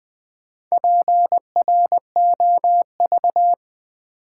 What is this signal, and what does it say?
Morse code, heard as a keyed tone, about 20 wpm, 700 Hz. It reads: PROV
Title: Build A Word - 4 Letters At A Time - 10-Letter Words - 20wpm